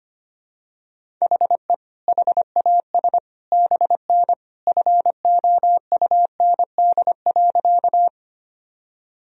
Morse code, heard as a keyed tone, about 25 wpm, 700 Hz. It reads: HE HAS BN FOUND.
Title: Sentences from Top 300 Words and Abbreviations 25wpm